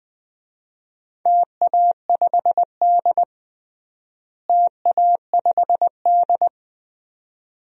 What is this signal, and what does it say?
Morse code, heard as a keyed tone, about 20 wpm, 700 Hz. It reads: TA5D TA5D
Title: DXCC-100 Call Signs - 4 Characters - Repeated 2x - 20wpm